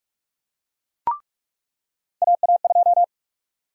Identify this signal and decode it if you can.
Morse code, heard as a keyed tone, about 44 wpm, 700 Hz. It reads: AA2
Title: USA Call Sign Prefixes - 45wpm